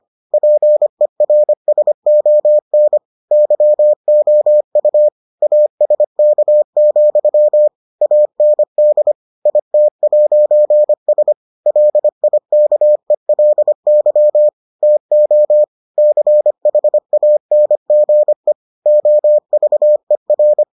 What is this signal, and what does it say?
Morse code, heard as a keyed tone, about 25 wpm, 600 Hz. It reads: PERSON YOU ASK, AND IT'S LIKELY TO CHANGE OVER